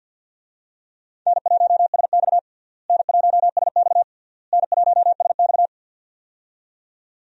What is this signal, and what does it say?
Morse code, heard as a keyed tone, about 52 wpm, 700 Hz. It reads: N1SX N1SX N1SX
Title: USA Call Signs - 4 Characters - Repeated 3x - 50wpm